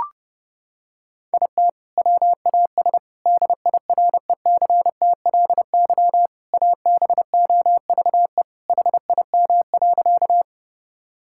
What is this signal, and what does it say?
Morse code, heard as a keyed tone, about 30 wpm, 700 Hz. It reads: IT WAS DIRECTLY ABOVE HIM.
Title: Sentences from Top 1000 Words - Review Part 5 - 30wpm